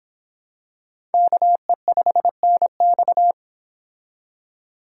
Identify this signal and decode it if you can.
Morse code, heard as a keyed tone, about 25 wpm, 700 Hz. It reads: KE5NX